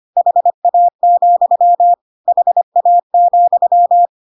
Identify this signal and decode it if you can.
Morse code, heard as a keyed tone, about 25 wpm, 700 Hz. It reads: HA, HA,
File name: Pinocchio / Pinocchio ch1 25 25 ik1hge